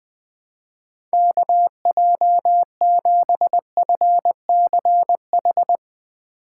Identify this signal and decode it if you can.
Morse code, heard as a keyed tone, about 21 wpm, 700 Hz. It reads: KJ7FCH
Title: Speed Racing - USA 2x3 Call Signs 17wpm